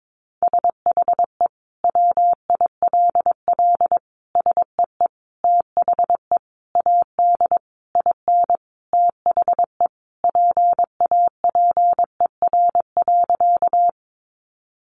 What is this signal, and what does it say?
Morse code, heard as a keyed tone, about 22 wpm, 700 Hz. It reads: SHE WILL SEE THE AD IN THE PAPER.
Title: Sentences from Top 1300 Words 22wpm